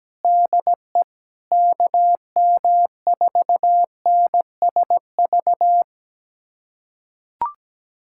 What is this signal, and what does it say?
Morse code, heard as a keyed tone, about 17 wpm, 700 Hz. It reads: DE KM4NSV
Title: USA Callsign DE Callsign - 6 Characters - 17wpm